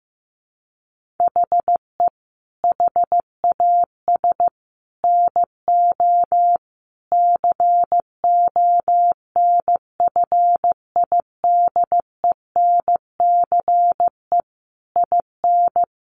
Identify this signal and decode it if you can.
Morse code, heard as a keyed tone, about 15 wpm, 700 Hz. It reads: HE HAS NO CONFIDENCE IN
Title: Sentences from Top 1500 Words 15wpm